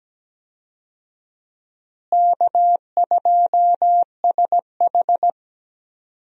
Morse code, heard as a keyed tone, about 17 wpm, 700 Hz. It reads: K2SH
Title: USA Call Signs - 4 Characters - Repeated 3x - 17wpm